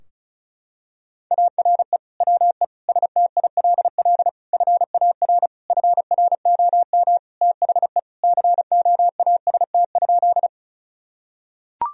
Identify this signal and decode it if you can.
Morse code, heard as a keyed tone, about 34 wpm, 700 Hz. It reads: ARE WE STILL FAR FROM THE COAST?
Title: Sentences from Top 1000 Words - Review Part 4 - 35wpm